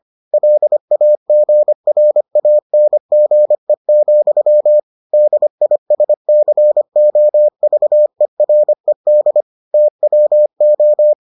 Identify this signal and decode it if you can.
Morse code, heard as a keyed tone, about 25 wpm, 600 Hz. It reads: LAGRANGE, DISCOVERED TWO